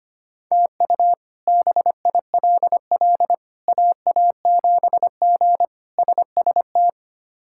Morse code, heard as a keyed tone, about 25 wpm, 700 Hz. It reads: TU BILL AA7G SST
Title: K1USN SST QSO Practice - 25wpm